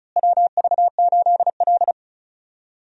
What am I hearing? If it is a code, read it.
Morse code, heard as a keyed tone, about 36 wpm, 700 Hz. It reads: WV8L
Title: USA Call Signs - 4 Characters - Repeated 2x - 35wpm